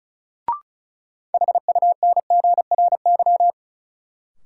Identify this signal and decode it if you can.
Morse code, heard as a keyed tone, about 36 wpm, 700 Hz. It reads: HUNGRY
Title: Sentences from Top 1400 Words - New Word Drill - 35wpm